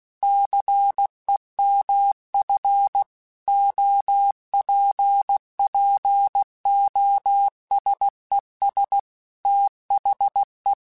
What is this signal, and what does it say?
Morse code, heard as a keyed tone, about 16 wpm, 800 Hz. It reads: CEMF OPPOSES THE
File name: Technical Subjects / Technical Subjects-mp3s / XL @15